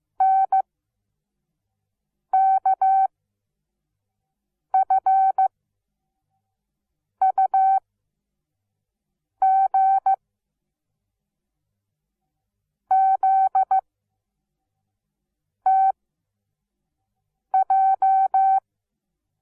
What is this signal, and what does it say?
Morse code, heard as a keyed tone, about 15 wpm, 770 Hz. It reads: N K F U G Z T J
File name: Onlinelehrgang / morsen / DJ4UF Morsen17